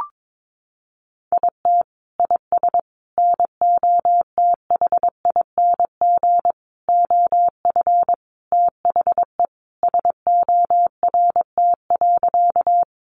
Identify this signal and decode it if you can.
Morse code, heard as a keyed tone, about 22 wpm, 700 Hz. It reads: IT IS NOTHING OF THE SORT.
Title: Sentences from Top 1300 Words 22wpm